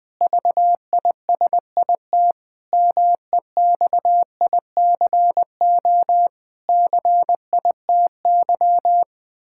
Morse code, heard as a keyed tone, about 20 wpm, 700 Hz. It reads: VISIT MEXICO CITY
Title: Pangram Challenge v2 - 20wpm